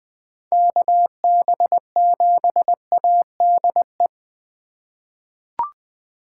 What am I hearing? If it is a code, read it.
Morse code, heard as a keyed tone, about 20 wpm, 700 Hz. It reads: KB7ADE
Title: USA Call Signs - 6 Characters - 3x Character Spacing - 20wpm